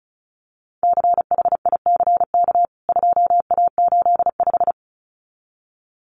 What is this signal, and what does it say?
Morse code, heard as a keyed tone, about 34 wpm, 700 Hz. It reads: CHICK 2A85
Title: CWT Exchange 35wpm